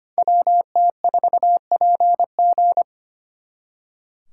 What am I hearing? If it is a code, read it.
Morse code, heard as a keyed tone, about 32 wpm, 700 Hz. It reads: WT4PG